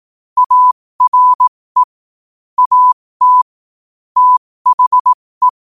Morse code, heard as a keyed tone, about 18 wpm, spaced as for 15 wpm, 1000 Hz. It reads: ARE AT THE
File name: qotdfiles / 15wpm